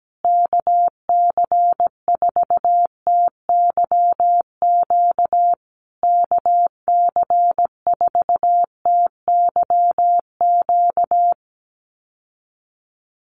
Morse code, heard as a keyed tone, about 17 wpm, 700 Hz. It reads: KC4TYQ KC4TYQ
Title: USA Call Signs Repeated 3x 17wpm